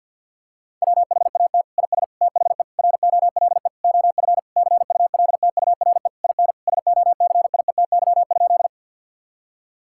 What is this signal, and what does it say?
Morse code, heard as a keyed tone, about 50 wpm, 700 Hz. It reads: WHAT IS THE ROLE OF CULTURE IN SOCIETY?
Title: Sentences from Top 1000 Words - Review Part 6 - 50wpm